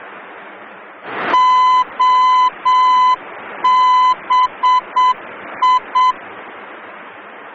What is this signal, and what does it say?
Morse code, heard as a keyed tone, about 7 wpm, 1010 Hz. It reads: OBI
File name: germany / mx / OBI-D